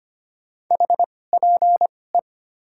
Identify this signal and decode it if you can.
Morse code, heard as a keyed tone, about 25 wpm, spaced as for 18 wpm, 700 Hz. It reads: HPE
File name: Top 90 QSO Elements - 2x Character Spacing - 25wpm